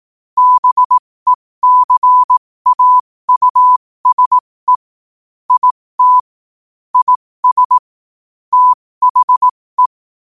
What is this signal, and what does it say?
Morse code, heard as a keyed tone, about 18 wpm, spaced as for 15 wpm, 1000 Hz. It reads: BECAUSE IT IS THE